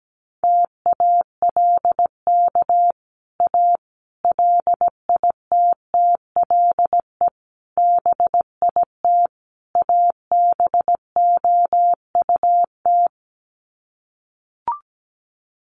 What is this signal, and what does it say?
Morse code, heard as a keyed tone, about 17 wpm, 700 Hz. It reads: TALK A LITTLE BIT ABOUT
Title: Sets of 5 Words Encore 17wpm